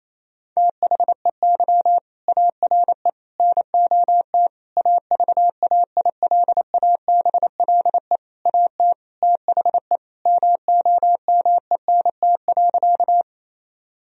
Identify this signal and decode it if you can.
Morse code, heard as a keyed tone, about 28 wpm, 700 Hz. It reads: THEY ARE NOT AVAILABLE AT THE MOMENT.
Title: Sentences from Top 1000 Words - Review Part 3 - 28wpm